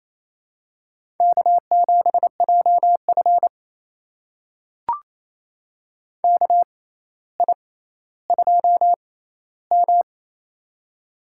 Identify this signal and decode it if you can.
Morse code, heard as a keyed tone, about 28 wpm, spaced as for 14 wpm, 700 Hz. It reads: K7JF KI2M
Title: USA Call Signs - 4 Characters - 6x Character Spacing - 28wpm